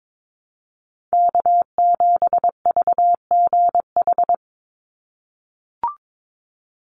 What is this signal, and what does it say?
Morse code, heard as a keyed tone, about 22 wpm, 700 Hz. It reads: K7VGH
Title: USA Call Signs 22wpm